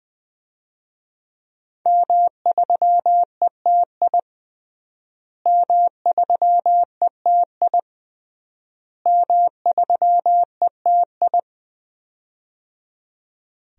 Morse code, heard as a keyed tone, about 21 wpm, 700 Hz. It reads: M3ETI M3ETI M3ETI
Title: DXCC-100 Call Signs - 5 Characters - Repeated 3x - 20wpm